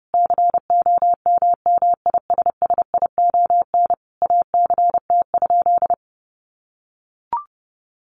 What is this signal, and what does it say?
Morse code, heard as a keyed tone, about 30 wpm, 700 Hz. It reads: COMMISSION ACT?
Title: Sentences from Top 1400 Words 30wpm